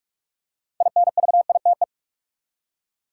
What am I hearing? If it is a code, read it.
Morse code, heard as a keyed tone, about 46 wpm, 700 Hz. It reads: INVITE
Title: Six Letter Words - 45wpm